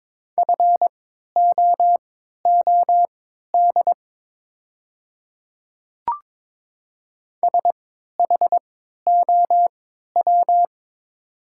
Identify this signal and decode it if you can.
Morse code, heard as a keyed tone, about 22 wpm, spaced as for 13 wpm, 700 Hz. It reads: FOOD SHOW
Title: 3x Character Spacing - Top 500 Words - 22wpm